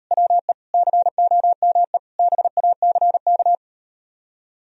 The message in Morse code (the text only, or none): WE COME BACK